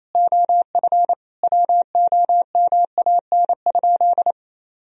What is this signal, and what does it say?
Morse code, heard as a keyed tone, about 28 wpm, 700 Hz. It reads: OF WOMAN?
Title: Sentences from Top 1000 Words - Review Part 3 - 28wpm